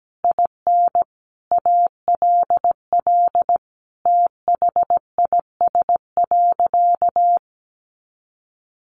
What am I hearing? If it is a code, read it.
Morse code, heard as a keyed tone, about 17 wpm, 700 Hz. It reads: IN ALL THIS.
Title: Sentences from Top 1400 Words 17wpm